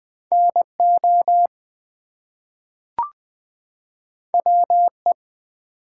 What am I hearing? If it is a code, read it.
Morse code, heard as a keyed tone, about 20 wpm, 700 Hz. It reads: NO WE
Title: Two Letter Words - 20wpm